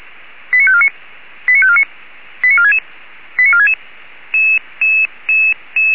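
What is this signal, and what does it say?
Five-tone selective call (ZVEI), heard as quick runs of five tones, tones of about 70 ms. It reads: 89549 89549 89466 89466
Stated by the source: ZVEI five-tone selective call 89544 received via SDR.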